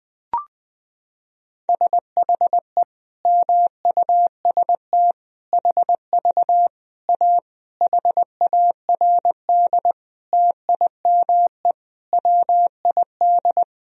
Morse code, heard as a keyed tone, about 21 wpm, 700 Hz. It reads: SHE MUST HV A HARD TIME WID
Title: Sentences from Top 300 Words and Abbreviations 20wpm